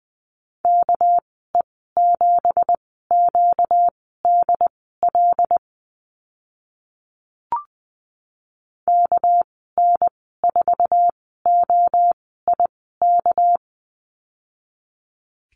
Morse code, heard as a keyed tone, about 20 wpm, spaced as for 14 wpm, 700 Hz. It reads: KE7QDL KN4OIK
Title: 245 Lesson 39g - Letter z - Call Signs 20wpm